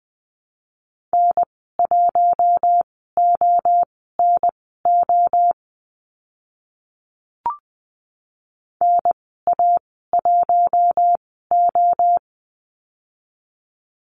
Morse code, heard as a keyed tone, about 20 wpm, spaced as for 14 wpm, 700 Hz. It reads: N1ONO NA1O
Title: 048 Lesson 9g - Number 4 - Call Signs 20wpm